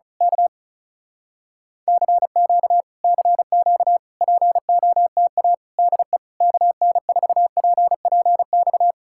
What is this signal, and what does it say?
Morse code, heard as a keyed tone, about 33 wpm, 700 Hz. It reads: K CQ CQ POTA DE KN4PPX
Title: Speed-Racing Common QSO Phrases Encore - 35wpm